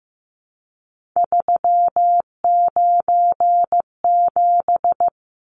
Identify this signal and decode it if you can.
Morse code, heard as a keyed tone, about 15 wpm, 700 Hz. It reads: 397
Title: Numbers - 3-Digit 15wpm